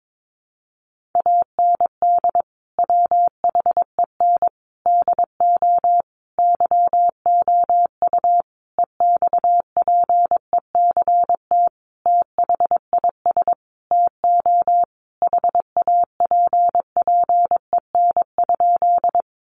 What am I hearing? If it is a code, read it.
Morse code, heard as a keyed tone, about 22 wpm, 700 Hz. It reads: AND WHEN DO YOU EXPECT THIS TO HAPPEN?
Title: Sentences from Top 1100 Words 22wpm